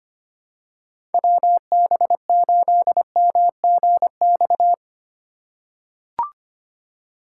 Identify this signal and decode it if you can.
Morse code, heard as a keyed tone, about 25 wpm, 700 Hz. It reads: WB8MGX
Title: USA Call Signs - 6 Characters - 2x Character Spacing - 25wpm